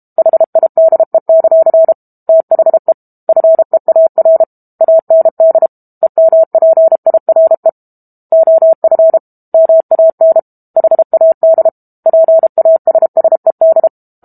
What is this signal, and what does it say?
Morse code, heard as a keyed tone, about 32 wpm, 650 Hz. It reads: HIDE; THE FEAR AND EMPIRE OF MAN HAD PASSED
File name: Book Two / 06-Chapter 6 - 32 WPM